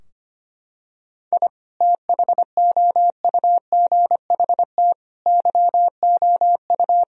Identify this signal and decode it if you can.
Morse code, heard as a keyed tone, about 25 wpm, 700 Hz. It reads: I THOUGHT YOU
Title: Sentences from Top 1000 Words - Review Part 4 - 25wpm